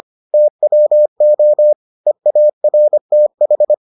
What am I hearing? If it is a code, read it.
Morse code, heard as a keyed tone, about 25 wpm, 600 Hz. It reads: TWO EARTH